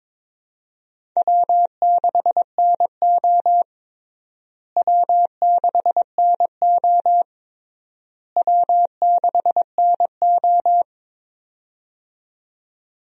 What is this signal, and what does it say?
Morse code, heard as a keyed tone, about 22 wpm, 700 Hz. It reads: W6NO W6NO W6NO
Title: USA Call Signs - 4 Characters - Repeated 3x - 22wpm